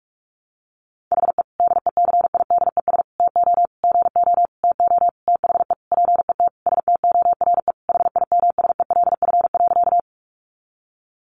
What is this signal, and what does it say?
Morse code, heard as a keyed tone, about 44 wpm, 700 Hz. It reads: HE DECIDES TO GO TO THE PET STORE HIMSELF.
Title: Easy Phrases 45wpm